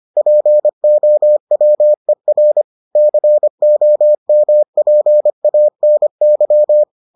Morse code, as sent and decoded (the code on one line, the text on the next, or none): .--. --- .-- . .-. / -.-. --- -- .--. .- -. -.--
POWER COMPANY